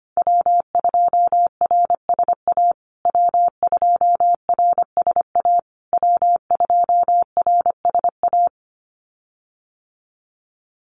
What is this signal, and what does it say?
Morse code, heard as a keyed tone, about 25 wpm, 700 Hz. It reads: W2RSA W2RSA W2RSA
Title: USA Call Signs Repeated 3x 25wpm